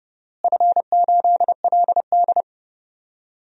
Morse code, heard as a keyed tone, about 30 wpm, 700 Hz. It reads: F8LD
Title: DXCC-100 Call Signs - 4 Characters - Repeated 3x - 30wpm